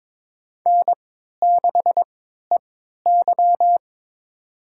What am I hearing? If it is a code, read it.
Morse code, heard as a keyed tone, about 22 wpm, spaced as for 13 wpm, 700 Hz. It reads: N6EY
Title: USA Call Signs - 4 Characters - 3x Character Spacing - 22wpm